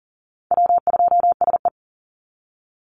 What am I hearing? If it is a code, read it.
Morse code, heard as a keyed tone, about 39 wpm, 700 Hz. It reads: W2SE
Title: NA SOTA Chaser Callsigns 40wpm